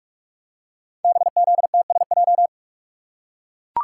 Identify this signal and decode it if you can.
Morse code, heard as a keyed tone, about 44 wpm, 700 Hz. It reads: BZTSJ
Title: Code Groups - 5 Letters - 45wpm